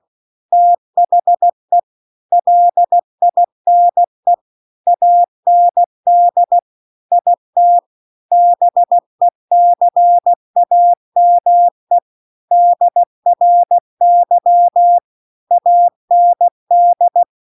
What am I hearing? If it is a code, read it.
Morse code, heard as a keyed tone, about 16 wpm, 700 Hz. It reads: THE LINE AND IT BECAME DRY AND
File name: CW-The-Big-Twoheaded-River-2-16WPM